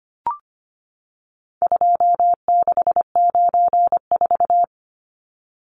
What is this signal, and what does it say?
Morse code, heard as a keyed tone, about 25 wpm, 700 Hz. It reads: 2694
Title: Numbers - 4-Digit 25wpm